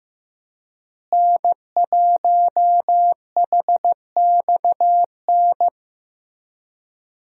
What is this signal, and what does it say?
Morse code, heard as a keyed tone, about 15 wpm, 700 Hz. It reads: N1HXN
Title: USA Call Signs - 5 Characters - Repeated 3x - 15wpm